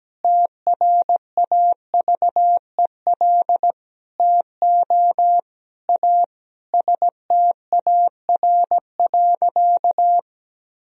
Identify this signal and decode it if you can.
Morse code, heard as a keyed tone, about 17 wpm, 700 Hz. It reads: TRAVEL TO A STAR.
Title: Sentences from Top 1000 Words - Review Part 2 - 17wpm